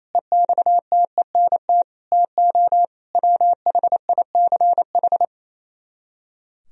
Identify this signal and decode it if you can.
Morse code, heard as a keyed tone, about 35 wpm, 700 Hz. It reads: EXTENT TO WHICH
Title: Speed-Racing - Sets of 4 Words 28wpm